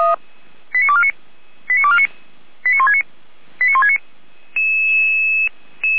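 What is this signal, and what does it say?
Five-tone selective call (ZVEI), heard as quick runs of five tones, tones of about 70 ms. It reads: 89249 89249 89169 89169
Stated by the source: Five-tone ZVEI selective call, received by SDR.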